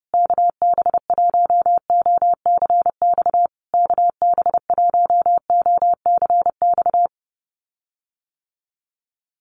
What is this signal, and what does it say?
Morse code, heard as a keyed tone, about 30 wpm, 700 Hz. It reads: KB1OCX KB1OCX
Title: USA Call Signs Repeated 3x 30wpm